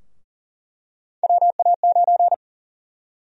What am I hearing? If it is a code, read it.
Morse code, heard as a keyed tone, about 39 wpm, 700 Hz. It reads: WA9
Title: USA Call Sign Prefixes - Repeated 2x - 40wpm